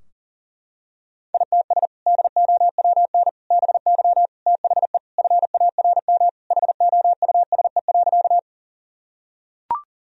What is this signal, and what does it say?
Morse code, heard as a keyed tone, about 40 wpm, 700 Hz. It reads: ITS DOWN BY THE FARM HOUSE.
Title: Sentences from Top 1000 Words - Review Part 2 - 40wpm